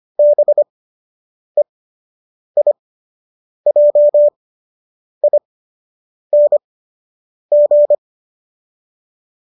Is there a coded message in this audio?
Yes: Morse code from 0.2 to 7.9 s, heard as a keyed tone, about 25 wpm, spaced as for 8 wpm, 590 Hz. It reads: BEIJING